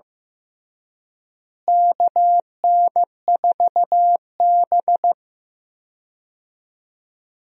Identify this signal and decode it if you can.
Morse code, heard as a keyed tone, about 15 wpm, 700 Hz. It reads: KN4B
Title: USA Call Signs - 4 Characters - Repeated 3x - 15wpm